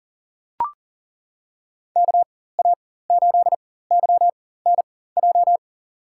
Morse code, heard as a keyed tone, about 40 wpm, spaced as for 19 wpm, 700 Hz. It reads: KA8YNJ
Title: USA Call Signs - 4x Character Spacing - 40wpm